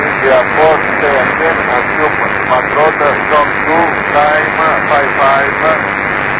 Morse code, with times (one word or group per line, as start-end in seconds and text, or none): none